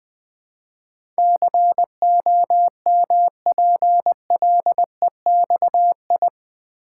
Morse code, heard as a keyed tone, about 20 wpm, 700 Hz. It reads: COMPLEXI
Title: Build A Word - 4 Letters At A Time - 10-Letter Words - 20wpm